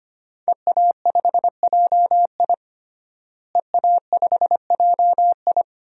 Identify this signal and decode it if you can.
Morse code, heard as a keyed tone, about 25 wpm, 700 Hz. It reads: EA5JI EA5JI
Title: DXCC-100 Call Signs - 5 Characters - Repeated 3x - 25wpm